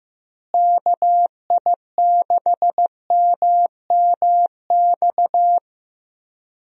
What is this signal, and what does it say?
Morse code, heard as a keyed tone, about 15 wpm, 700 Hz. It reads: KI6MMX
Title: USA Call Signs - 6 Characters - Repeated 3x - 15wpm